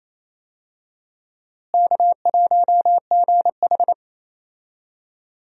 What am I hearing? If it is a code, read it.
Morse code, heard as a keyed tone, about 28 wpm, 700 Hz. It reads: K1GH
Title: USA Call Signs - 4 Characters - Repeated 3x - 28wpm